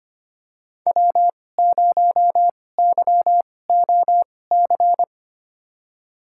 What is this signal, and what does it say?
Morse code, heard as a keyed tone, about 25 wpm, spaced as for 18 wpm, 700 Hz. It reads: W0YOC
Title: USA Call Signs - 5 Characters - 2x Character Spacing - 25wpm